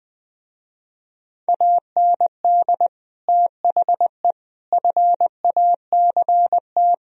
Morse code, heard as a keyed tone, about 21 wpm, 700 Hz. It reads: AND THE FACT
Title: Speed-Racing - Sets of 4 Words 20wpm